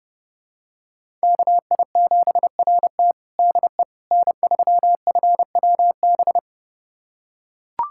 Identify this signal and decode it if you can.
Morse code, heard as a keyed tone, about 30 wpm, 700 Hz. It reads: KI7RT DE N3FWB
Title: USA Callsign DE Callsign - 5 Characters - 30wpm